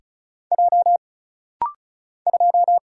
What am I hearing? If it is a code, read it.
Morse code, heard as a keyed tone, about 36 wpm, 700 Hz. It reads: J 2
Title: Single Letter-Number Alt - Rapid-Fire 35wpm